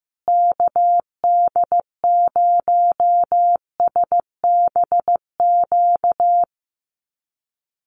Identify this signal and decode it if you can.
Morse code, heard as a keyed tone, about 15 wpm, 700 Hz. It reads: KD0SBQ
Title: USA Call Signs Repeated 3x 15wpm